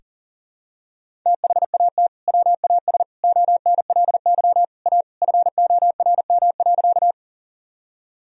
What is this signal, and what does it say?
Morse code, heard as a keyed tone, about 40 wpm, 700 Hz. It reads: THAT WAS ONLY A FORM.
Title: Sentences from Top 1000 Words - Review Part 1 - 40wpm